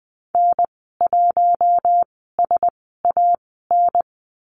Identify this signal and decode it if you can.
Morse code, heard as a keyed tone, about 20 wpm, spaced as for 14 wpm, 700 Hz. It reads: N1SAN
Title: 042 Lesson 8g - Number 1 - Call Signs 20wpm